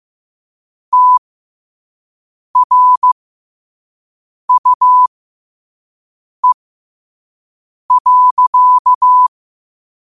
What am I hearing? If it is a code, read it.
Morse code, heard as a keyed tone, about 15 wpm, spaced as for 5 wpm, 1000 Hz. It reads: TRUE.
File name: qotdfiles / 5wpm